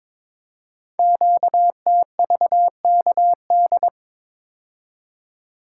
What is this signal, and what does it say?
Morse code, heard as a keyed tone, about 22 wpm, 700 Hz. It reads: QTVKD
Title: Code Groups - 5 Letter:Numbers - 22wpm